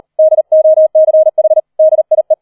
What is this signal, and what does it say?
Morse code, heard as a keyed tone, about 38 wpm, 630 Hz. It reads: DOCH DIE
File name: 004 Vierter Streich 40wpm